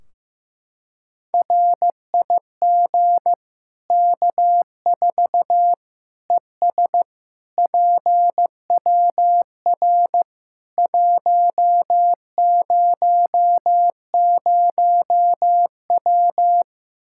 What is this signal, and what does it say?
Morse code, heard as a keyed tone, about 15 wpm, 700 Hz. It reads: RIG K4 ES PWR 100W
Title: Practice QSO - Exchange 2 - Info - 15wpm